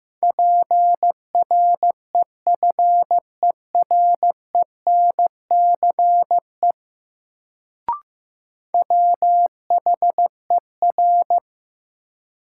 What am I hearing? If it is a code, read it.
Morse code, heard as a keyed tone, about 15 wpm, 700 Hz. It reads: PREFERENCE WHER
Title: Sentences from Top 1500 Words - New Word Drill - 15wpm